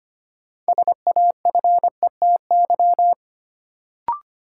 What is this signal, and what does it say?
Morse code, heard as a keyed tone, about 25 wpm, 700 Hz. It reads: SAFETY
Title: Sentences from Top 700 Words - New Word Drill - 25wpm